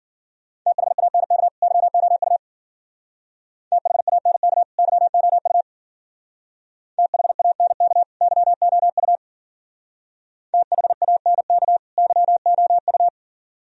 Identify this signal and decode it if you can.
Morse code, heard as a keyed tone, about 50 wpm, 700 Hz. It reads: THANK YOU THANK YOU THANK YOU THANK YOU
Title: Speed-Racing - Sets of 2 Words 40wpm